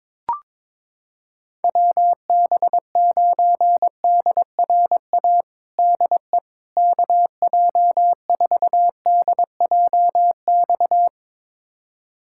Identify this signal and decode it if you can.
Morse code, heard as a keyed tone, about 22 wpm, 700 Hz. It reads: WB9DRA DE KJ4DJX
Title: USA Callsign DE Callsign - 6 Characters - 22wpm